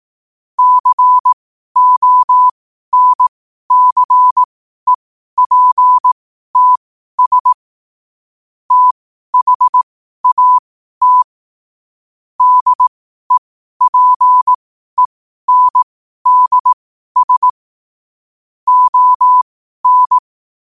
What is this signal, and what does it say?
Morse code, heard as a keyed tone, about 18 wpm, spaced as for 12 wpm, 1000 Hz. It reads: CONCEPTS THAT DEPENDS ON
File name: qotdfiles / 12wpm